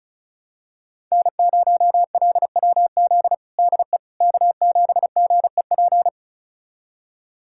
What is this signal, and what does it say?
Morse code, heard as a keyed tone, about 36 wpm, 690 Hz. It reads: N0LWZ DE K7GEP
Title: USA Callsign DE Callsign - 5 Characters - 35wpm